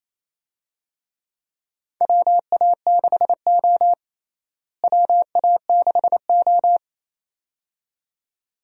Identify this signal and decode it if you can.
Morse code, heard as a keyed tone, about 28 wpm, 700 Hz. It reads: WA6O WA6O
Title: USA Call Signs - 4 Characters - Repeated 2x - 28wpm